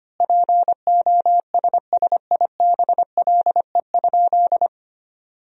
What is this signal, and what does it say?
Morse code, heard as a keyed tone, about 25 wpm, 700 Hz. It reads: POSSIBLE?
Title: 3x Word-Spacing - Sentences from Top 500 Words - 25wpm